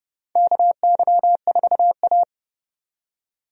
Morse code, heard as a keyed tone, about 29 wpm, 700 Hz. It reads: KY4A
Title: USA Call Signs - 4 Characters - 30wpm